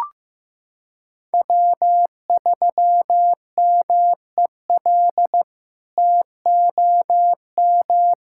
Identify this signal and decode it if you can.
Morse code, heard as a keyed tone, about 15 wpm, 700 Hz. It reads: W3MEL TOM
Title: K1USN SST QSO Practice - 15wpm